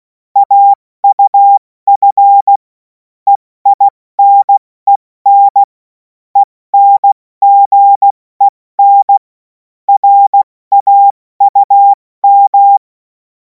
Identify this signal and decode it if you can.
Morse code, heard as a keyed tone, about 16 wpm, 800 Hz. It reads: AUF EINEN ENGEN RAUM
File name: Morsekurs / Witze / Witz 14. April 2026 70 F80